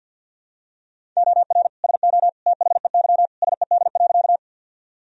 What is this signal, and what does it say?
Morse code, heard as a keyed tone, about 47 wpm, 700 Hz. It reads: OR SO THEY SED.
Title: Sentences from Top 100 Words and Abbreviations 50wpm